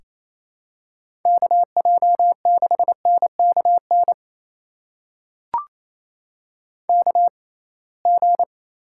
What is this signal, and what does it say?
Morse code, heard as a keyed tone, about 28 wpm, spaced as for 21 wpm, 700 Hz. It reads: KJ6NKN KG
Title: USA Call Signs - 6x Character Spacing - 28wpm